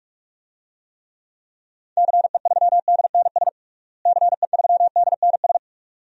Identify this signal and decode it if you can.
Morse code, heard as a keyed tone, about 43 wpm, 700 Hz. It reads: CE3DNS CE3DNS
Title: DXCC-100 Call Signs - 6 Characters - Repeated 2x - 45wpm